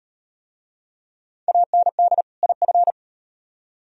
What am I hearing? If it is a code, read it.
Morse code, heard as a keyed tone, about 38 wpm, 700 Hz. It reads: AND IF